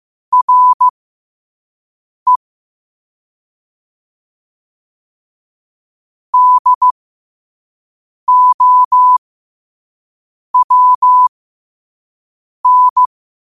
Morse code, heard as a keyed tone, about 15 wpm, spaced as for 5 wpm, 1000 Hz. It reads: RE DOWN
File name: qotdfiles / 5wpm